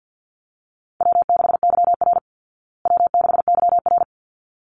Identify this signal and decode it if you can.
Morse code, heard as a keyed tone, about 49 wpm, 700 Hz. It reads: W6YR W6YR
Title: Call Signs Repeated 3x - 50wpm